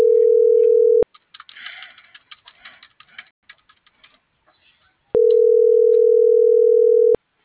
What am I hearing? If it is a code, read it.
Ringback tone (North America), 440 + 480 Hz.
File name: CENTRAL BLN / out ericka.canales